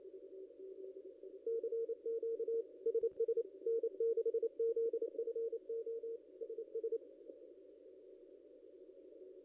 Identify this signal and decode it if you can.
Morse code, heard as a keyed tone, about 28 wpm, 460 Hz. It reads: CQ SS N6ZFO SS
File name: contests / ss-cw-k5tr-op-n5rz -0700